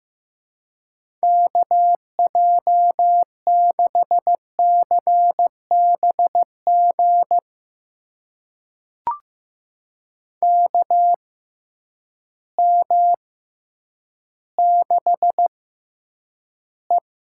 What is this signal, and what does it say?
Morse code, heard as a keyed tone, about 15 wpm, spaced as for 9 wpm, 700 Hz. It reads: KJ6CBG KM6E
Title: USA Call Signs - 6 Characters - 6x Character Spacing - 15wpm